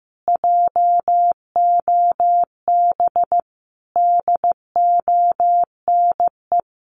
Morse code, heard as a keyed tone, about 15 wpm, 700 Hz. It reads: JOB DONE
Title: Sets of 5 Words Encore2 15wpm